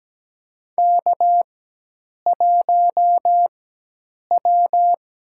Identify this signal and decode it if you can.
Morse code, heard as a keyed tone, about 17 wpm, spaced as for 8 wpm, 700 Hz. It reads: K1W
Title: USA Call Signs - 4x Character Spacing - 17wpm